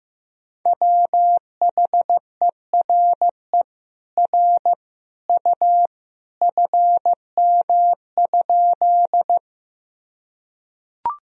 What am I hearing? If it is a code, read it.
Morse code, heard as a keyed tone, about 15 wpm, 700 Hz. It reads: WHERE R U FM?
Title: Sentences from Top 200 Words and Abbreviations 15wpm